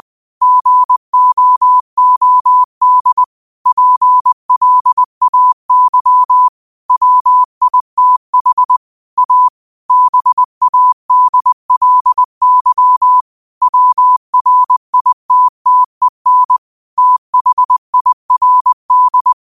Morse code, heard as a keyed tone, about 20 wpm, 1000 Hz. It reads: GOOD PLAY WITH A BADLY WRITTEN THIRD
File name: qotdfiles / 20wpm